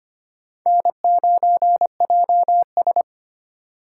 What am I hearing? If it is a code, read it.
Morse code, heard as a keyed tone, about 25 wpm, 700 Hz. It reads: N9JS
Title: USA Call Signs - 4 Characters - 4x Character Spacing - 25wpm